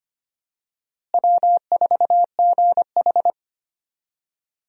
Morse code, heard as a keyed tone, about 25 wpm, 700 Hz. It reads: W4GH
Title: USA Call Signs - 4 Characters - Repeated 3x - 25wpm